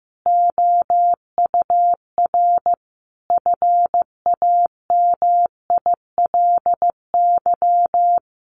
Morse code, heard as a keyed tone, about 15 wpm, 700 Hz. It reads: OUR FAMILY
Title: Sentences from Top 1200 Words 15wpm